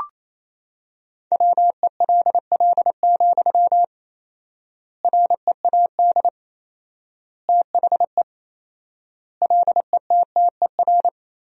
Morse code, heard as a keyed tone, about 28 wpm, 700 Hz. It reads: WELL, READ THE LETTER
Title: 4x Word-Spacing - Sentences from Top 300 Words - 28wpm